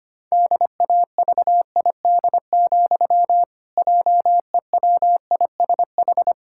Morse code, heard as a keyed tone, about 25 wpm, 700 Hz. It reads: DAVID, JEWISH
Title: Pangram Challenge v2 - 25wpm